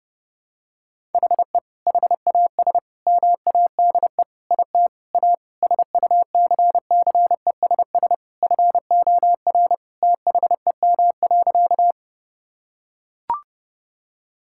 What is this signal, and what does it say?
Morse code, heard as a keyed tone, about 30 wpm, 700 Hz. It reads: HE HAS MADE IT A SUCCESS FOR THEM.
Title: Sentences from Top 1000 Words - Review Part 6 - 30wpm